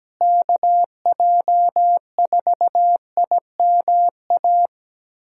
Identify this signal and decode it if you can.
Morse code, heard as a keyed tone, about 17 wpm, 700 Hz. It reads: KJ4IMA
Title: USA Callsign DE Callsign - 6 Characters - 17wpm